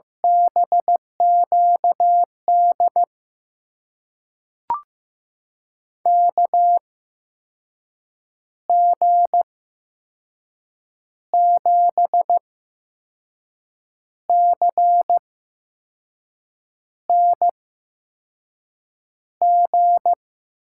Morse code, heard as a keyed tone, about 15 wpm, spaced as for 5 wpm, 700 Hz. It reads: BQD KG7CNG
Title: USA Call Signs - 6 Characters - 8x Character Spacing - 15wpm